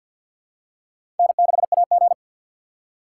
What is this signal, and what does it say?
Morse code, heard as a keyed tone, about 51 wpm, 700 Hz. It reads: N6AG
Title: USA Call Signs - 6x Character Spacing - 50wpm